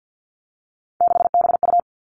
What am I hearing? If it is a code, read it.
Morse code, heard as a keyed tone, about 50 wpm, 700 Hz. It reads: 6BU
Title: Three Challenging Letter-Number Combinations 50wpm